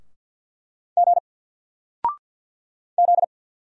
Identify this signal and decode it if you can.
Morse code, heard as a keyed tone, about 48 wpm, 700 Hz. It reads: G Z
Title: Single Character (Letter:Number:Punctuation:Prosign) 50wpm